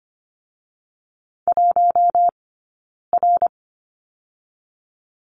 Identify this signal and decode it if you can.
Morse code, heard as a keyed tone, about 25 wpm, 700 Hz. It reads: 1 R
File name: Lesson 17d - Letter u - Two Characters 25wpm